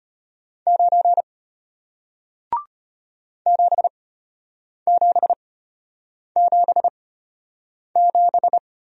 Morse code, heard as a keyed tone, about 34 wpm, spaced as for 8 wpm, 700 Hz. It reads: 9 7777